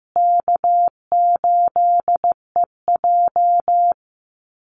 Morse code, heard as a keyed tone, about 15 wpm, 700 Hz. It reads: K8EJ
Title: Easy Call Signs Repeated 2x - 15wpm